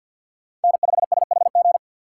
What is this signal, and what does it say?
Morse code, heard as a keyed tone, about 50 wpm, 700 Hz. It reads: N5SHG
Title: USA Call Signs - 5 Characters - 4x Character Spacing - 50wpm